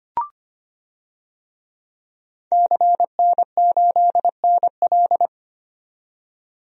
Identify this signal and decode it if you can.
Morse code, heard as a keyed tone, about 25 wpm, 700 Hz. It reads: CN8NL
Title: DXCC-100 Call Signs - 5 Characters - 25wpm